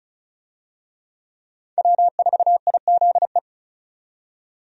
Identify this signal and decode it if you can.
Morse code, heard as a keyed tone, about 36 wpm, 700 Hz. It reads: W4IZE
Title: USA Call Signs - 5 Characters - 35wpm